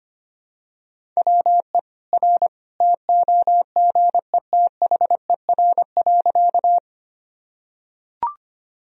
Morse code, heard as a keyed tone, about 25 wpm, 700 Hz. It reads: WE R TOGETHER.
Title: Sentences from Top 300 Words and Abbreviations 25wpm